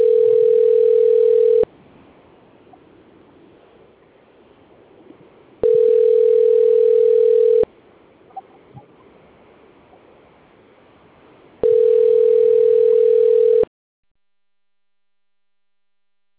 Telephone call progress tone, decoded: ringback tone

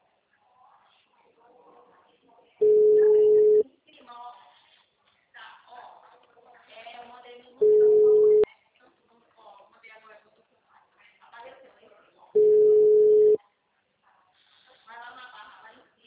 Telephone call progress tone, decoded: ringback tone

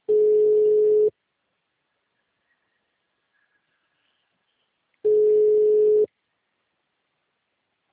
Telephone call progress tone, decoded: ringback tone

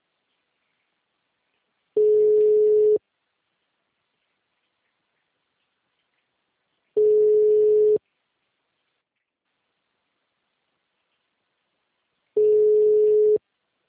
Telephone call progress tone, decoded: ringback tone

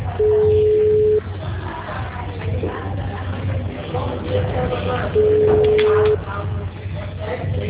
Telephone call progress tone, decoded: ringback tone

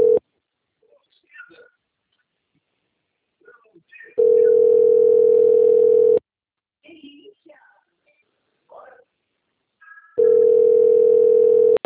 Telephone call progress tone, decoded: ringback tone

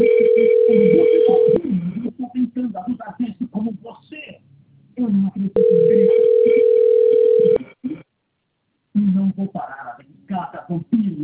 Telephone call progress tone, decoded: ringback tone

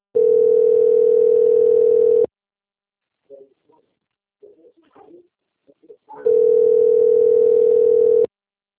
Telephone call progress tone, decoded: ringback tone

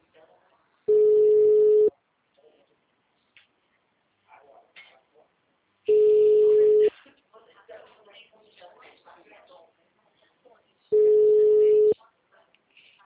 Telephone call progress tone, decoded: ringback tone

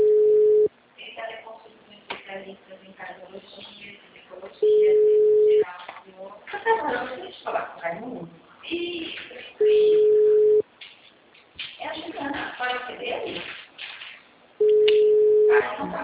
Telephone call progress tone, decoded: ringback tone

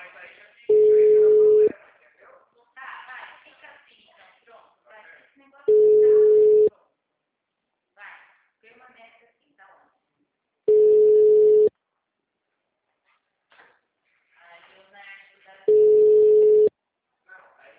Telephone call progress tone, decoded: ringback tone

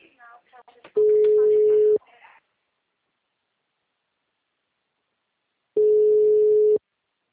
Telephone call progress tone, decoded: ringback tone

